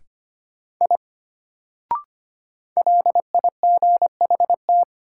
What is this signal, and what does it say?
Morse code, heard as a keyed tone, about 25 wpm, 700 Hz. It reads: I LIGHT
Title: Sentences from Top 200 Words - New Word Drill - 25wpm